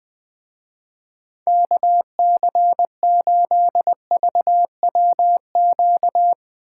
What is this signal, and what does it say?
Morse code, heard as a keyed tone, about 20 wpm, 700 Hz. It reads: KC8VWQ
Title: Speed Racing - USA 2x3 Call Signs 17wpm